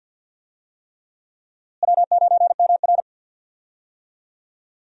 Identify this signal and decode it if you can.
Morse code, heard as a keyed tone, about 51 wpm, 700 Hz. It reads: W9MR
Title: USA Call Signs - 4 Characters - Repeated 3x - 50wpm